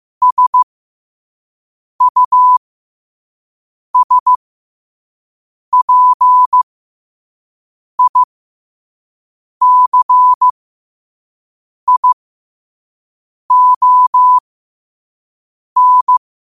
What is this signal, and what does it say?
Morse code, heard as a keyed tone, about 15 wpm, spaced as for 5 wpm, 1000 Hz. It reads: SUSPICION
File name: qotdfiles / 5wpm